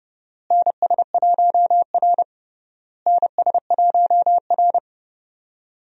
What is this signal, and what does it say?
Morse code, heard as a keyed tone, about 30 wpm, 700 Hz. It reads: NS1R NS1R
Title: USA Call Signs - 4 Characters - Repeated 3x - 30wpm